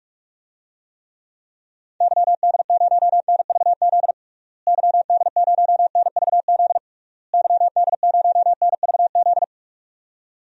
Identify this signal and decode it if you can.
Morse code, heard as a keyed tone, about 42 wpm, 700 Hz. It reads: YD0NVZ YD0NVZ YD0NVZ
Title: DXCC-100 Call Signs - 6 Characters - Repeated 3x - 45wpm